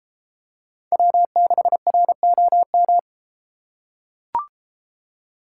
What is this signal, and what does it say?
Morse code, heard as a keyed tone, about 33 wpm, 700 Hz. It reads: W6ROM